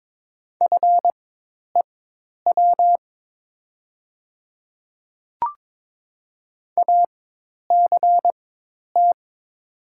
Morse code, heard as a keyed tone, about 22 wpm, spaced as for 10 wpm, 700 Hz. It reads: FEW ACT
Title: 4x Character Spacing - Top 300 Words - 22wpm